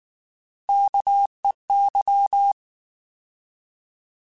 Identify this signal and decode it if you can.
Morse code, heard as a keyed tone, about 19 wpm, 800 Hz. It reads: KEY
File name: Alphabetical / Alphabetical mp3s / Ks @18